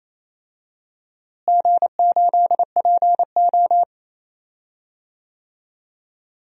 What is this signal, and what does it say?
Morse code, heard as a keyed tone, about 28 wpm, 700 Hz. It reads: G8PO
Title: DXCC-100 Call Signs - 4 Characters - Repeated 3x - 28wpm